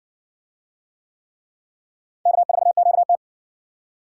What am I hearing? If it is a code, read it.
Morse code, heard as a keyed tone, about 58 wpm, 700 Hz. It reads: K4QT